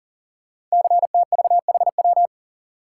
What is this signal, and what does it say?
Morse code, heard as a keyed tone, about 40 wpm, 700 Hz. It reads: CTVHW
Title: Code Groups - 5 Letters - 40wpm